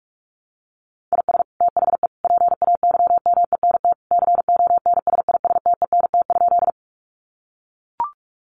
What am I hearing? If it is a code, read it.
Morse code, heard as a keyed tone, about 44 wpm, 700 Hz. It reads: IS THE PAYMENT CONSISTENT?
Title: Sentences from Top 1500 Words 45wpm